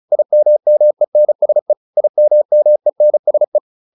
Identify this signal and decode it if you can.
Morse code, heard as a keyed tone, about 34 wpm, 600 Hz. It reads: IMMENSE IMMENSE